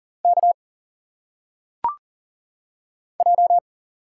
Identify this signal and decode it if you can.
Morse code, heard as a keyed tone, about 40 wpm, 700 Hz. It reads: K J
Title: Single Letters - Phonetic Pronunciation 40wpm